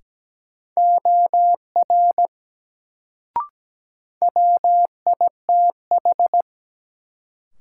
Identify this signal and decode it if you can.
Morse code, heard as a keyed tone, about 17 wpm, 700 Hz. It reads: OR WITH
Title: Sentences from Top 100 Words - New Word Drill - 17wpm